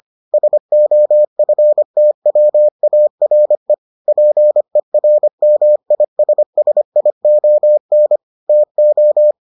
Morse code, heard as a keyed tone, about 25 wpm, 600 Hz. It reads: SOFTWARE PERMISSION TO